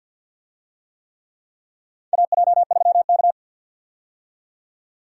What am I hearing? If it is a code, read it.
Morse code, heard as a keyed tone, about 47 wpm, 700 Hz. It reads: AJ3K
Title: USA Call Signs - 4 Characters - 50wpm